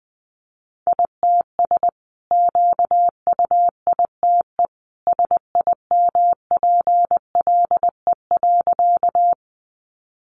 Morse code, heard as a keyed tone, about 20 wpm, 700 Hz. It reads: ITS QUITE SIMPLE.
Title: Sentences from Top 700 Words 20wpm